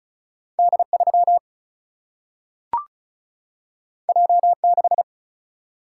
Morse code, heard as a keyed tone, about 36 wpm, 700 Hz. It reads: D3 J6
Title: Single-Letter Single-Number 35wpm